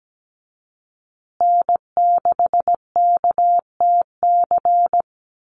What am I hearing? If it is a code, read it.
Morse code, heard as a keyed tone, about 17 wpm, 700 Hz. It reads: N6KTC
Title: USA Call Signs Repeated 3x 17wpm